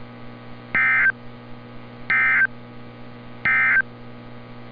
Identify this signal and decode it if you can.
Emergency Alert System (SAME) tones, heard as harsh buzzing bursts. It is heard from 0.6 to 3.9 s.
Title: NOAA Weather Radio - Altoona, PA